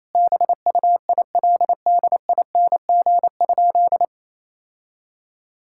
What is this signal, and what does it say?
Morse code, heard as a keyed tone, about 27 wpm, 700 Hz. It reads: BUILDING?
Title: Sentences from Top 1000 Words - Review Part 5 - 28wpm